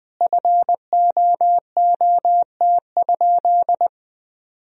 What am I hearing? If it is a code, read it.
Morse code, heard as a keyed tone, about 20 wpm, 700 Hz. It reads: FOOT?
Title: Sentences from Top 1000 Words - Review Part 3 - 20wpm